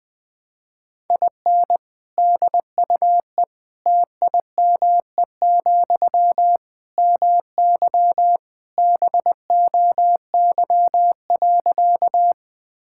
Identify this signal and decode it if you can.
Morse code, heard as a keyed tone, about 21 wpm, 700 Hz. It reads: IN DUE TIME, MY BOY.
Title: Sentences from Top 1000 Words - Review Part 6 - 20wpm